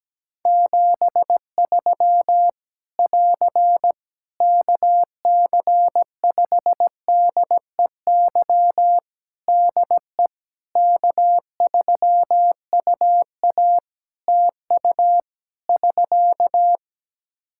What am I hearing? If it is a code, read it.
Morse code, heard as a keyed tone, about 17 wpm, 700 Hz. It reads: 73 <AR> KC5DEY DE K3UA TU <SK>
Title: Practice QSO - Exchange 4 - Ending - 17wpm